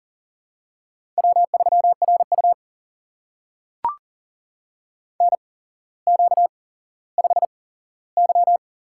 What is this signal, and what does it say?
Morse code, heard as a keyed tone, about 40 wpm, spaced as for 17 wpm, 700 Hz. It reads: W3RU NQ5Y
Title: USA Call Signs - 4 Characters - 8x Character Spacing - 40wpm